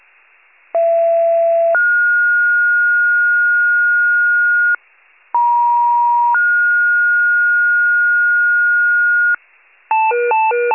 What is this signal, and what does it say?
Two-tone page, heard as a one-second tone, then a three-second tone, about 670 then 1430 Hz, about 950 then 1430 Hz. It is heard from 0.7 to 9.3 s.
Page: 39 FIRE